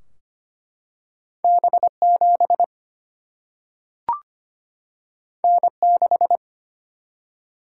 Morse code, heard as a keyed tone, about 25 wpm, 700 Hz. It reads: B7 N6
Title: Single-Letter Single-Number 25wpm